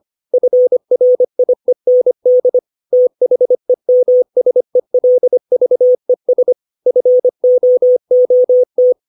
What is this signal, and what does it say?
Morse code, heard as a keyed tone, about 25 wpm, 500 Hz. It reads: FRIEND THEMSELVES FOOT
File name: 632 words / 632 Words 25WPM